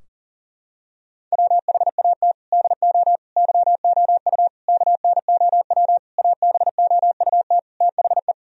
Morse code, heard as a keyed tone, about 40 wpm, 700 Hz. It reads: WHAT DO YOU KNOW ABOUT THE
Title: Sentences from Top 1000 Words - Review Part 3 - 40wpm